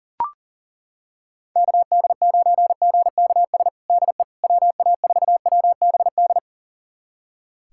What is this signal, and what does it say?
Morse code, heard as a keyed tone, about 37 wpm, 700 Hz. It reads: KD9GKS DE WA4WBD
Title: USA Callsign DE Callsign - 6 Characters - 40wpm